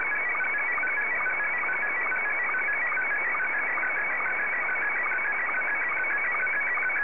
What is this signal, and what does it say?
SSTV (slow-scan television), Scottie 1.